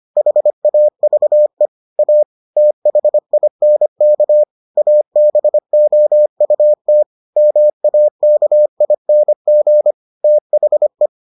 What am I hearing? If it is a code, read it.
Morse code, heard as a keyed tone, about 25 wpm, 600 Hz. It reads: HAVE A THINK ABOUT MAKING THE